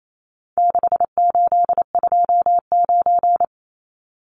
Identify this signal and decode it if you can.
Morse code, heard as a keyed tone, about 28 wpm, 700 Hz. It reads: 6829
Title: Numbers - 4-Digit 28wpm